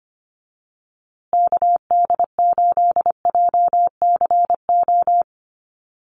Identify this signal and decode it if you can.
Morse code, heard as a keyed tone, about 25 wpm, 700 Hz. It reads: KD8JCO
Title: USA Call Signs Repeated 3x 25wpm